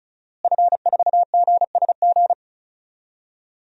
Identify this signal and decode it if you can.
Morse code, heard as a keyed tone, about 34 wpm, 700 Hz. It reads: F4GSG
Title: DXCC-100 Call Signs - 5 Characters - Repeated 3x - 35wpm